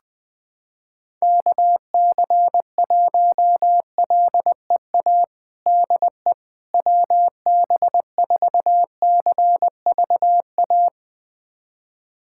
Morse code, heard as a keyed tone, about 20 wpm, 700 Hz. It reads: KC1LEA DE WB4CVA
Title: USA Callsign DE Callsign 20wpm